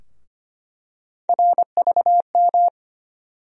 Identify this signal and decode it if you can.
Morse code, heard as a keyed tone, about 25 wpm, 700 Hz. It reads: RVM
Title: USA Call Sign Postfixes - 3 Letters 25wpm